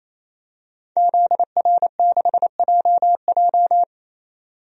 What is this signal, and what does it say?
Morse code, heard as a keyed tone, about 28 wpm, 700 Hz. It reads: ZR6JJ
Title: DXCC-100 Call Signs - 5 Characters - 28wpm